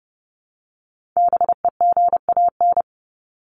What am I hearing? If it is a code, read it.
Morse code, heard as a keyed tone, about 30 wpm, 700 Hz. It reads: BEGAN
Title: Top 300 Words 30wpm